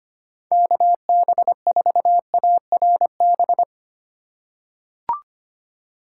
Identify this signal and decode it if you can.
Morse code, heard as a keyed tone, about 25 wpm, 700 Hz. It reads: KB4ARB
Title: USA Call Signs - 4x Character Spacing - 25wpm